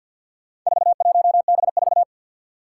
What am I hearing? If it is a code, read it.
Morse code, heard as a keyed tone, about 47 wpm, 700 Hz. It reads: 41B4
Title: Four Challenging Letter-Number Combinations 50wpm